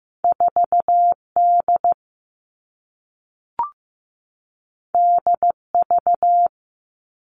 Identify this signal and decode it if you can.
Morse code, heard as a keyed tone, about 15 wpm, 700 Hz. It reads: 4 D D V
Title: Two Challenging Letter-Number Combinations 15wpm